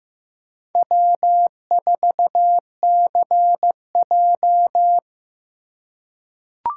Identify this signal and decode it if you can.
Morse code, heard as a keyed tone, about 15 wpm, 700 Hz. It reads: W4CJ
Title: USA Call Signs - 4 Characters - 3x Character Spacing - 15wpm